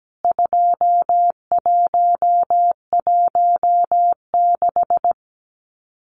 Morse code, heard as a keyed tone, about 17 wpm, 700 Hz. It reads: 2116
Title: CWT Exchange 17wpm